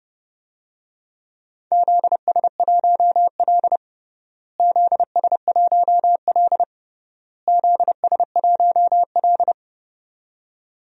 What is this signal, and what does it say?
Morse code, heard as a keyed tone, about 30 wpm, 700 Hz. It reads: ZS1L ZS1L ZS1L
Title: DXCC-100 Call Signs - 4 Characters - Repeated 3x - 30wpm